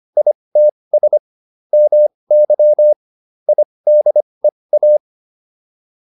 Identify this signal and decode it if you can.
Morse code, heard as a keyed tone, about 25 wpm, spaced as for 20 wpm, 600 Hz. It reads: ITS MY IDEA